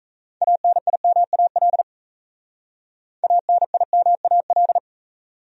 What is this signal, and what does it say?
Morse code, heard as a keyed tone, about 34 wpm, 700 Hz. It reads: ANIMAL ANIMAL
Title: Speed-Racing Top 200 Words 28wpm